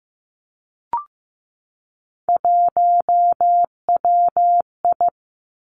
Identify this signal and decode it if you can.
Morse code, heard as a keyed tone, about 15 wpm, 700 Hz. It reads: 1WI
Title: Three Challenging Letter-Number Combinations 15wpm